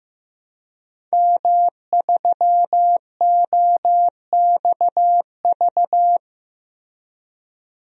Morse code, heard as a keyed tone, about 15 wpm, 700 Hz. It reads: M3OXV
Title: DXCC-100 Call Signs - 5 Characters - Repeated 3x - 15wpm